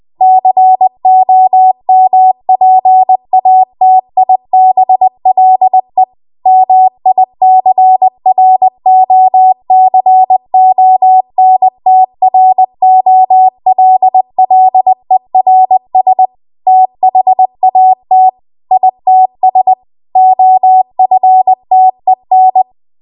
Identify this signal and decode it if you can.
Morse code, heard as a keyed tone, about 20 wpm, 750 Hz. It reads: COMPATIBLE MICROCONTROLLERS THAT ITS OFTEN